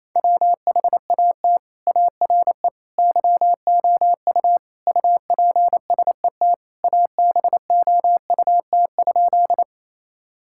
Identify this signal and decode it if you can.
Morse code, heard as a keyed tone, about 28 wpm, 700 Hz. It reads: WHAT ARE YOU UPSET ABOUT?
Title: Sentences from Top 1000 Words - Review Part 5 - 28wpm